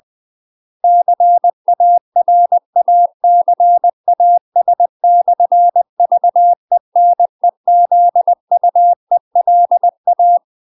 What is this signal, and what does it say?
Morse code, heard as a keyed tone, about 20 wpm, 700 Hz. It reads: CARACAS/VENEZUELA
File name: UK MASTER QSOs TRAINING / 20WPM / UK MASTER QSOs 0032